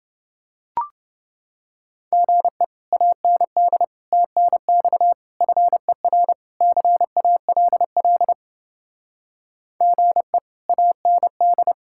Morse code, heard as a keyed tone, about 29 wpm, 700 Hz. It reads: GE AND TNX FER CALL GE AND